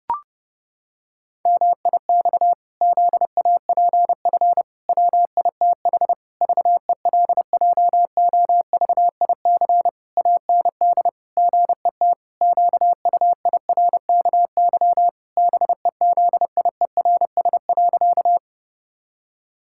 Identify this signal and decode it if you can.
Morse code, heard as a keyed tone, about 30 wpm, 700 Hz. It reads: MIX ZAPF WITH VELJOVIC AND GET QUIRKY BEZIERS.
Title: Pangram Challenge v2 - 30wpm